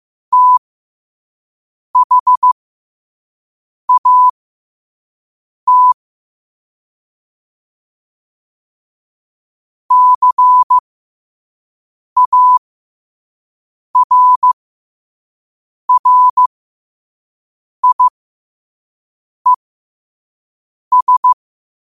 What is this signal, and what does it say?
Morse code, heard as a keyed tone, about 15 wpm, spaced as for 5 wpm, 1000 Hz. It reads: THAT CARRIES
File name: qotdfiles / 5wpm